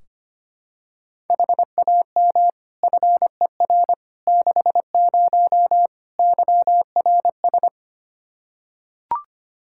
Morse code, heard as a keyed tone, about 25 wpm, 700 Hz. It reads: HAM FER 60 YRS
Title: Practice QSO - Exchange 3 - More Info - 25wpm